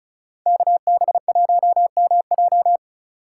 Morse code, heard as a keyed tone, about 46 wpm, 700 Hz. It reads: KB1MJ